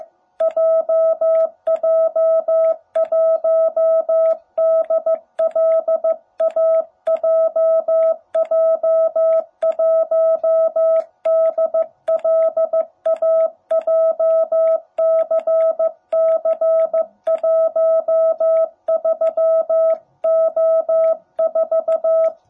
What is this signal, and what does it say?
Morse code, heard as a keyed tone, about 15 wpm, 660 Hz. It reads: JJ1DLA JJ1DLA JCC13O4
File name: jh8cbh / 2021cw